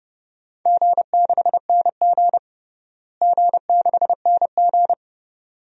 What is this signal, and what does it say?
Morse code, heard as a keyed tone, about 30 wpm, 700 Hz. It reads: G6NG G6NG
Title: DXCC-100 Call Signs - 4 Characters - Repeated 2x - 30wpm